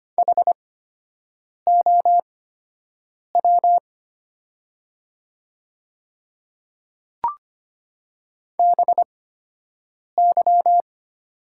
Morse code, heard as a keyed tone, about 25 wpm, spaced as for 7 wpm, 700 Hz. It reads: HOW BY
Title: 8x Character Spacing - Top 300 Words - 25wpm